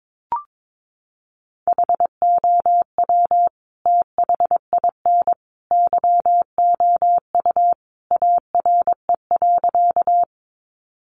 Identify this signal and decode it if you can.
Morse code, heard as a keyed tone, about 22 wpm, 700 Hz. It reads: HOW THIN YOU ARE.
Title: Sentences from Top 1200 Words 22wpm